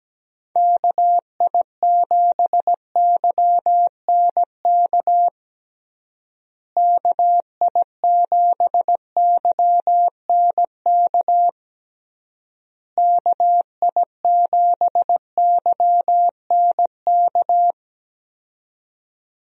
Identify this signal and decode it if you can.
Morse code, heard as a keyed tone, about 17 wpm, 700 Hz. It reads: KI7YNK KI7YNK KI7YNK
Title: USA Call Signs - 6 Characters - Repeated 3x - 17wpm